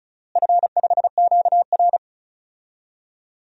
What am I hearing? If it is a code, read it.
Morse code, heard as a keyed tone, about 36 wpm, 700 Hz. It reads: F5QR
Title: DXCC-100 Call Signs - 4 Characters - Repeated 2x - 35wpm